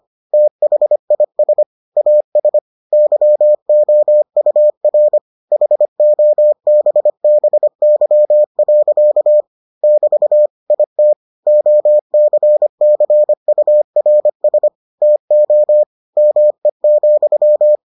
Morse code, heard as a keyed tone, about 25 wpm, 600 Hz. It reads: THIS AS YOUR HOBBY. <BT> IT OCCURS TO ME,